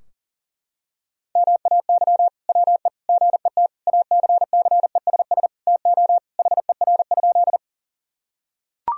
Morse code, heard as a keyed tone, about 37 wpm, 700 Hz. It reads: MAY WE GET ACCESS TO HER?
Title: Sentences from Top 1000 Words - Review Part 4 - 40wpm